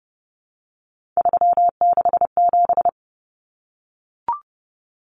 Morse code, heard as a keyed tone, about 30 wpm, 700 Hz. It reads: 367
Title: Common Cut Numbers - 3-Digit 30wpm